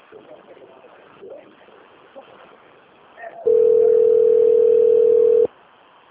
Ringback tone (North America), 440 + 480 Hz.